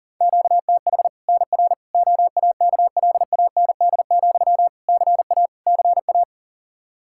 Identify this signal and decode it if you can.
Morse code, heard as a keyed tone, about 40 wpm, 700 Hz. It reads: QTH NR OAKLAND, CA CA
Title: K1USN SST QSO Practice 40wpm